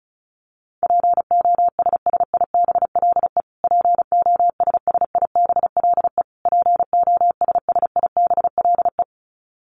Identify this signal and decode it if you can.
Morse code, heard as a keyed tone, about 34 wpm, 700 Hz. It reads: POSSIBLE POSSIBLE POSSIBLE
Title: Top 500 Words Repeated 35wpm